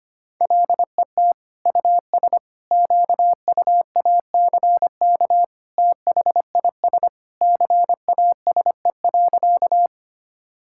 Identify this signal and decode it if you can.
Morse code, heard as a keyed tone, about 25 wpm, 700 Hz. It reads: LET US QUACK THIS CASE.
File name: practice / 25wpm / All-Age Appropriate Jokes - 25wpm